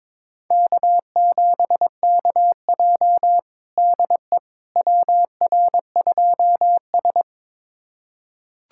Morse code, heard as a keyed tone, about 21 wpm, 700 Hz. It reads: K7KJ DE WR2S
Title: USA Callsign DE Callsign - 4 Characters - 22wpm